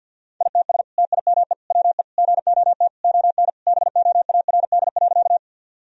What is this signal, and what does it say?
Morse code, heard as a keyed tone, about 50 wpm, 700 Hz. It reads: ITS TIME WE GOT ON BOARD.
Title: Sentences from Top 1000 Words - Review Part 4 - 50wpm